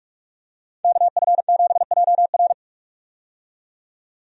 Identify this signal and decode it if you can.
Morse code, heard as a keyed tone, about 46 wpm, 700 Hz. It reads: KF7JR
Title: USA Call Signs - 5 Characters - 45wpm